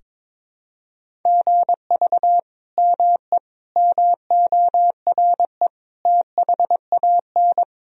Morse code, heard as a keyed tone, about 22 wpm, 700 Hz. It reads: GV ME MORE THAN
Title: Sentences from Top 200 Words and Abbreviations 22wpm